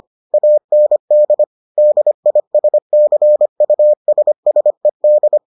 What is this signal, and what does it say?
Morse code, heard as a keyed tone, about 25 wpm, 600 Hz. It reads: AND DISCUSSED